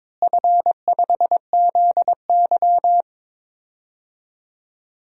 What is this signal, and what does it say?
Morse code, heard as a keyed tone, about 22 wpm, 700 Hz. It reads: F5ZY
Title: DXCC-100 Call Signs - 4 Characters - Repeated 2x - 22wpm